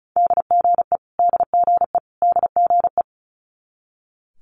Morse code, heard as a keyed tone, about 36 wpm, 700 Hz. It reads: DGE DGE DGE
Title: Multi-letter Phonemes Repeated 3x 35wpm